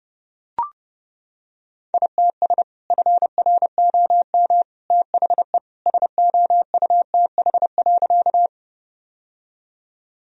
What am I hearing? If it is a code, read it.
Morse code, heard as a keyed tone, about 32 wpm, 700 Hz. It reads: ITS FROM THE SOUTH.
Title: Sentences from Top 1000 Words - Review Part 2 - 30wpm